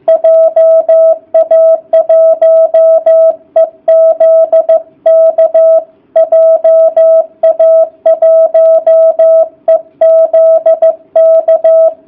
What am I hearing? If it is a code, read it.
Morse code, heard as a keyed tone, about 15 wpm, 660 Hz. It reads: JA1EZK JA1EZK